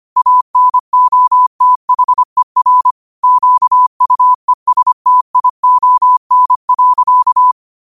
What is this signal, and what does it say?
Morse code, heard as a keyed tone, about 25 wpm, 1000 Hz. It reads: ANOTHER QUESTION.
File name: qotdfiles / 25wpm